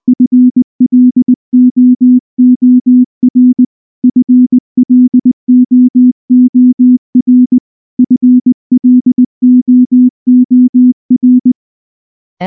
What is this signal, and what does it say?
Morse code, heard as a keyed tone, about 19 wpm, 250 Hz. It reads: FLOOR FLOOR FLOOR
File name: morse / morse-training-20WPM-1